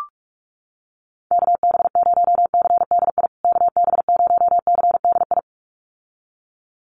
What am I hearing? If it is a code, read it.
Morse code, heard as a keyed tone, about 44 wpm, 700 Hz. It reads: KB0CDI KB0CDI
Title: USA Call Signs Repeated 2x 45wpm